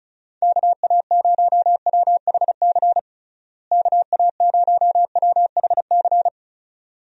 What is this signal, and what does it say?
Morse code, heard as a keyed tone, about 34 wpm, 700 Hz. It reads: KA0WHC KA0WHC
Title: USA Call Signs - 6 Characters - Repeated 2x - 35wpm